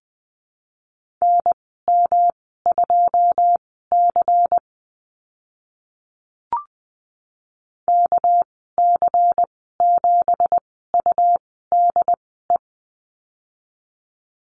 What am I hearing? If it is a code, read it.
Morse code, heard as a keyed tone, about 20 wpm, spaced as for 14 wpm, 700 Hz. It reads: NM2C KC7UDE
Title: 213 Lesson 34g - Number 8 - Call Signs 20wpm